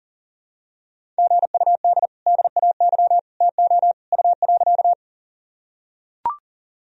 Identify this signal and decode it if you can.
Morse code, heard as a keyed tone, about 37 wpm, 700 Hz. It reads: GUD DAY TO U.
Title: Sentences from Top 200 Words and Abbreviations 40wpm